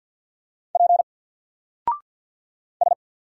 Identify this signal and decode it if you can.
Morse code, heard as a keyed tone, about 50 wpm, 700 Hz. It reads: P S
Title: Single Letter-Number Alt 50wpm